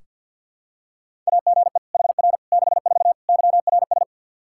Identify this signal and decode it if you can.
Morse code, heard as a keyed tone, about 46 wpm, 700 Hz. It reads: AGE HR 64 YRS
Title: K1USN SST QSO Practice 50wpm